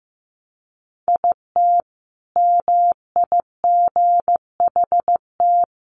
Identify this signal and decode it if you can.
Morse code, heard as a keyed tone, about 14 wpm, 700 Hz. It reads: IT MIGHT
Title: Sentences from Top 1200 Words 15wpm